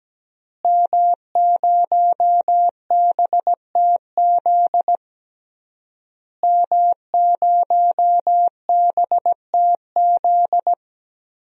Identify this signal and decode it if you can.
Morse code, heard as a keyed tone, about 17 wpm, 700 Hz. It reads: M0BTZ M0BTZ
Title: DXCC-100 Call Signs - 5 Characters - Repeated 3x - 17wpm